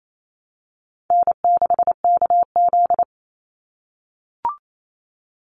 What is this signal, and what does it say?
Morse code, heard as a keyed tone, about 28 wpm, 700 Hz. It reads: N6KZ
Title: NA SOTA Chaser Callsigns 28wpm